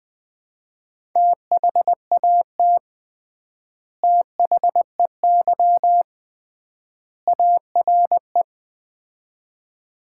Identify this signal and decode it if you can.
Morse code, heard as a keyed tone, about 20 wpm, 700 Hz. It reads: THAT THEY ARE
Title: 3x Word-Spacing - Sets of 3 Words - 20wpm